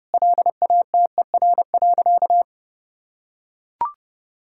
Morse code, heard as a keyed tone, about 30 wpm, 700 Hz. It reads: LATER.
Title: Sentences from Top 1000 Words - Review Part 6 - 30wpm